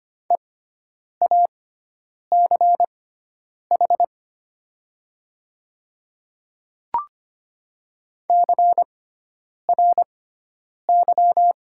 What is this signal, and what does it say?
Morse code, heard as a keyed tone, about 25 wpm, spaced as for 9 wpm, 700 Hz. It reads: EACH CRY
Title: 6x Character Spacing - Top 500 Words - 25wpm